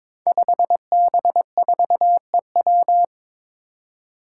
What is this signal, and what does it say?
Morse code, heard as a keyed tone, about 22 wpm, 700 Hz. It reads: 5B4EW
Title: Five Challenging Letter-Number Combinations 22wpm